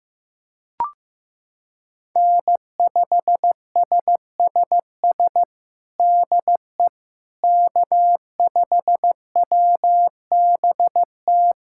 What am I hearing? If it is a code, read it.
Morse code, heard as a keyed tone, about 16 wpm, 700 Hz. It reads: N5SSS DE K5WBT
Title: USA Callsign DE Callsign - 5 Characters - 15wpm